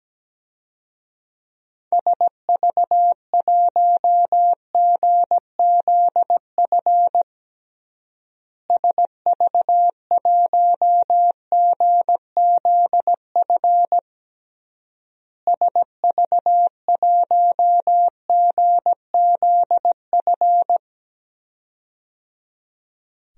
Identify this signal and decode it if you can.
Morse code, heard as a keyed tone, about 17 wpm, 700 Hz. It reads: SV1GZF SV1GZF SV1GZF
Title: DXCC-100 Call Signs - 6 Characters - Repeated 3x - 17wpm